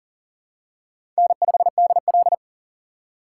Morse code, heard as a keyed tone, about 42 wpm, 700 Hz. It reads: N5DL
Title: USA Call Signs - 4 Characters - 8x Character Spacing - 40wpm